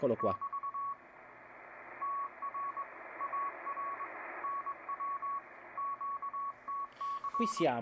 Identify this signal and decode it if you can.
Morse code, heard as a keyed tone, about 20 wpm, 1120 Hz, with speech. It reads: <BT> TR AR NW QTC